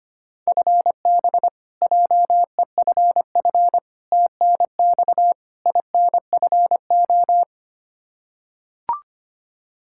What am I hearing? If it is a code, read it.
Morse code, heard as a keyed tone, about 25 wpm, 700 Hz. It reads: FB JEFF TNX INFO
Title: Practice QSO - Exchange 3 - More Info - 25wpm